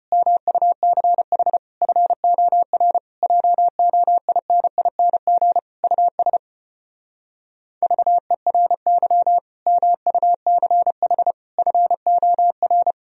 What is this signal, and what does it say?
Morse code, heard as a keyed tone, about 33 wpm, 700 Hz. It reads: MUCH FOR JOINING US VERY MUCH FOR